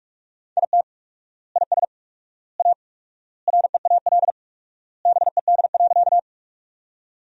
Morse code, heard as a keyed tone, about 45 wpm, 700 Hz. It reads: IT IS A REAL BED.
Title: 4x Word-Spacing - Sentences from Top 500 Words - 45wpm